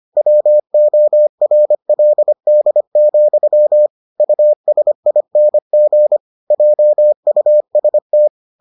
Morse code, heard as a keyed tone, about 25 wpm, 600 Hz. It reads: WORLD, USING JUST